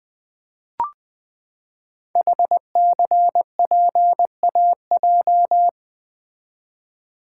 Morse code, heard as a keyed tone, about 20 wpm, 700 Hz. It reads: HCPAJ
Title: Code Groups - 5 Letters - 20wpm